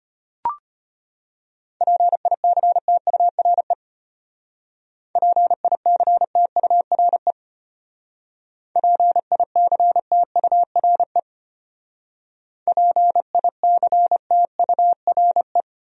Morse code, heard as a keyed tone, about 32 wpm, 700 Hz. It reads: PICTURE PICTURE PICTURE PICTURE